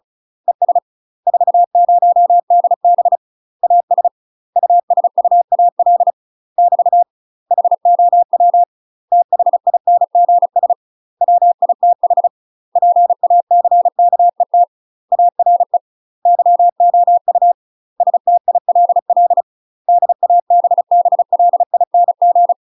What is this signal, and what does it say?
Morse code, heard as a keyed tone, about 36 wpm, 700 Hz. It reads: ES 40DB AS USUAL <BT> HOW THINGS WITH PACKET ARE YOU STILL DABBLING
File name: UK MASTER QSOs TRAINING / 35WPM / UK MASTER QSOs 0018